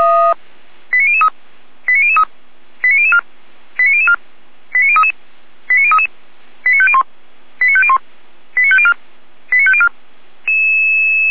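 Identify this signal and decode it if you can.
Five-tone selective call (ZVEI), heard as quick runs of five tones, tones of about 70 ms. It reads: 89003 89003 89004 89004 89033 89033 89501 89501 89504 89504
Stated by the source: ZVEI five-tone call received by SDR, from feuerwehr-orlamuende (fire brigade).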